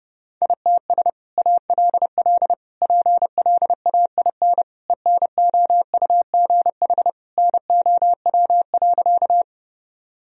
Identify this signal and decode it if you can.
Morse code, heard as a keyed tone, about 30 wpm, 700 Hz. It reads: ITS ALL PLAIN ENOUGH NOW.
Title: Sentences from Top 1000 Words - Review Part 2 - 30wpm